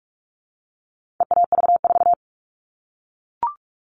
Morse code, heard as a keyed tone, about 44 wpm, 700 Hz. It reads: EAV4
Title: All Cut Numbers - 4-Digit 45wpm